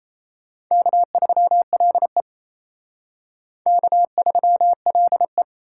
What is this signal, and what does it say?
Morse code, heard as a keyed tone, about 36 wpm, 700 Hz. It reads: K3LE K3LE